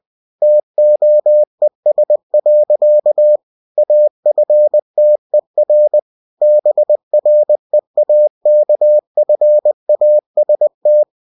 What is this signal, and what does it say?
Morse code, heard as a keyed tone, about 20 wpm, 600 Hz. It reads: TOES. AFTER BREAKFAST